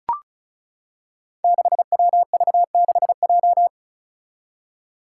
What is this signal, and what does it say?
Morse code, heard as a keyed tone, about 34 wpm, 700 Hz. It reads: 6WV6J
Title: Five Challenging Letter-Number Combinations 35wpm